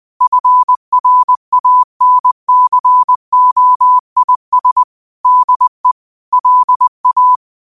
Morse code, heard as a keyed tone, about 20 wpm, 1000 Hz. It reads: FRANCOIS DE LA
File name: 20wpm